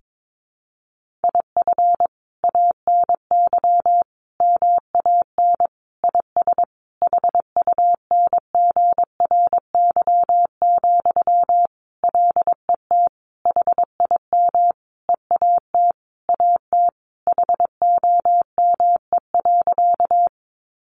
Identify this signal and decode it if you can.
Morse code, heard as a keyed tone, about 22 wpm, 700 Hz. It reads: IF ANY MAN IS HUNGRY, LET HIM EAT AT HOME.
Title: Sentences from Top 1400 Words 22wpm